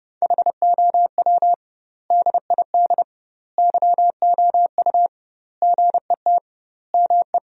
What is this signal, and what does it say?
Morse code, heard as a keyed tone, about 30 wpm, 700 Hz. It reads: HOW DID YOU GET ME
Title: 2x Word-Spacing - Sentences from Top 200 Words - 30wpm